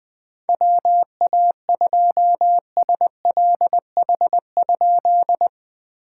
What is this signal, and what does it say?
Morse code, heard as a keyed tone, about 20 wpm, 690 Hz. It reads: WA2SLH?
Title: POTA QSO Practice - 20wpm